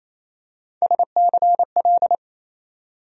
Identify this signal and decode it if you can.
Morse code, heard as a keyed tone, about 28 wpm, 700 Hz. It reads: SCL
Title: USA Call Sign Postfixes - 3 Letters 28wpm